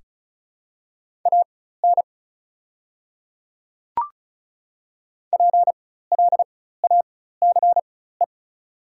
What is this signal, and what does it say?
Morse code, heard as a keyed tone, about 34 wpm, spaced as for 16 wpm, 700 Hz. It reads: AN PLACE
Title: 4x Character Spacing - Top 300 Words - 35wpm